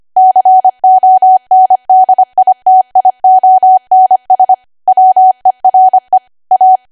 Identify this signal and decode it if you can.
Morse code, heard as a keyed tone, about 25 wpm, 750 Hz. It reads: CONDITIONS WERE A
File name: cwa / W1AW-Level3 / 25WPM